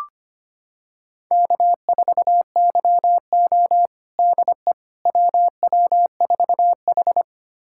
Morse code, heard as a keyed tone, about 25 wpm, 700 Hz. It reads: K4YO DE WW4H
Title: USA Callsign DE Callsign - 4 Characters - 25wpm